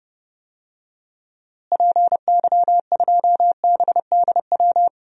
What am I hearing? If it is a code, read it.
Morse code, heard as a keyed tone, about 30 wpm, 700 Hz. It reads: PY2BDW
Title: DXCC-100 Call Signs - 6 Characters - Repeated 3x - 30wpm